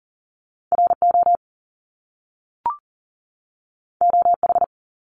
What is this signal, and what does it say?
Morse code, heard as a keyed tone, about 40 wpm, 700 Hz. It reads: RO OH
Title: Top Two Letter Combinations in Words 40wpm